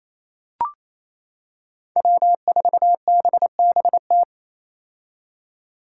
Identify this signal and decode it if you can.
Morse code, heard as a keyed tone, about 28 wpm, 700 Hz. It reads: W4BBT
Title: K1USN SST QSO Practice - 28wpm